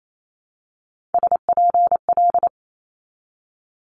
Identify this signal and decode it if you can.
Morse code, heard as a keyed tone, about 28 wpm, 700 Hz. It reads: SPL
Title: Multi-letter Phonemes 28wpm